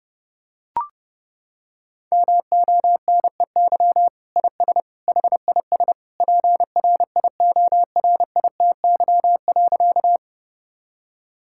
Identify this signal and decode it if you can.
Morse code, heard as a keyed tone, about 32 wpm, 700 Hz. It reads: MONEY IS HIS PRIORITY.
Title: Sentences from Top 1600 Words 30wpm